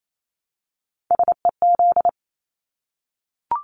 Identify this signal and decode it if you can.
Morse code, heard as a keyed tone, about 28 wpm, 700 Hz. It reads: SEZ
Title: Abbreviations Repeated 3x - 28wpm